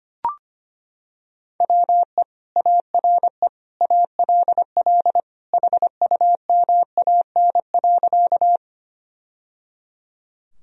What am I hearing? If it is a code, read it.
Morse code, heard as a keyed tone, about 25 wpm, 700 Hz. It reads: WE ARE ALL HUMAN.
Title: Sentences from Top 1000 Words - Review Part 3 - 25wpm